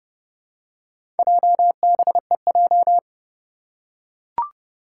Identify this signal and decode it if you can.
Morse code, heard as a keyed tone, about 30 wpm, 700 Hz. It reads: JBEJ
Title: Four Challenging Letter-Number Combinations 30wpm